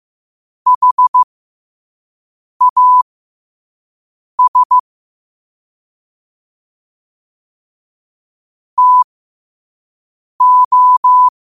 Morse code, heard as a keyed tone, about 15 wpm, spaced as for 5 wpm, 1000 Hz. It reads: HAS TO
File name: 2026-03-16-5wpm